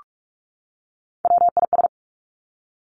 Morse code, heard as a keyed tone, about 46 wpm, 700 Hz. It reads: WIS
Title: Three Challenging Letter-Number Combinations 45wpm